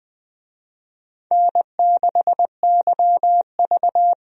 Morse code, heard as a keyed tone, about 20 wpm, 700 Hz. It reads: N6YV
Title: USA Call Signs - 4 Characters - 20wpm